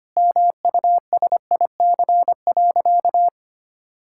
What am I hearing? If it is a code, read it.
Morse code, heard as a keyed tone, about 25 wpm, 700 Hz. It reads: MUSIC.
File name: practice / 25wpm / Sentences with 3-Word Phrases - 25wpm